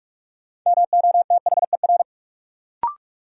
Morse code, heard as a keyed tone, about 46 wpm, 700 Hz. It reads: MOTHER
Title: Sentences from Top 200 Words - New Word Drill - 45wpm